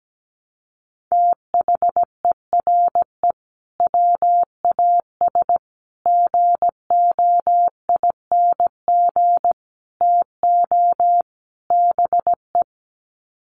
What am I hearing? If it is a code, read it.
Morse code, heard as a keyed tone, about 17 wpm, 700 Hz. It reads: THERE WAS GOING TO BE
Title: Sets of 5 Words Encore2 17wpm